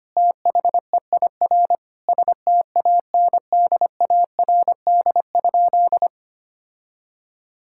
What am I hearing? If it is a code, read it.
Morse code, heard as a keyed tone, about 25 wpm, 700 Hz. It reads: THEIR STANDARD?
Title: Sentences from Top 1000 Words - Review Part 6 - 25wpm